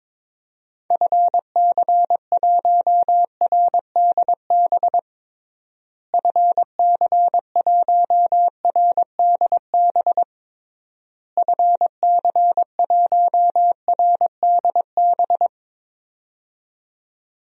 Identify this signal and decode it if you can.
Morse code, heard as a keyed tone, about 22 wpm, 700 Hz. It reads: FC1RDB FC1RDB FC1RDB
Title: DXCC-100 Call Signs - 6 Characters - Repeated 3x - 22wpm